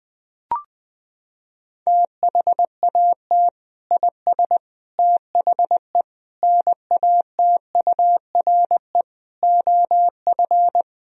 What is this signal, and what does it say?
Morse code, heard as a keyed tone, about 21 wpm, 700 Hz. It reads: THAT IS THE NATURE OF
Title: Sentences from Top 1000 Words - Review Part 3 - 20wpm